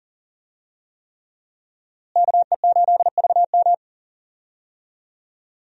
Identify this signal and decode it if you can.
Morse code, heard as a keyed tone, about 40 wpm, 700 Hz. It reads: KE8VM
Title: USA Call Signs - 5 Characters - Repeated 3x - 40wpm